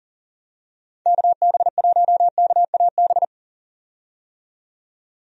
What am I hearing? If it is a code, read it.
Morse code, heard as a keyed tone, about 40 wpm, 700 Hz. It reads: KB1KAB
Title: USA Call Signs - 6 Characters - 40wpm